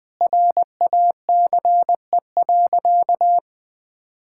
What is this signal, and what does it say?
Morse code, heard as a keyed tone, about 20 wpm, 700 Hz. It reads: RACE.
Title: Sentences with 8-Word Phrases - 20wpm